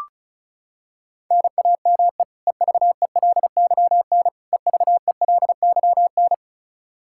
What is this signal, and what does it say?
Morse code, heard as a keyed tone, about 36 wpm, 700 Hz. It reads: NAME EVELYN EVELYN
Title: Practice QSO - Exchange 1 - Beginning - 35wpm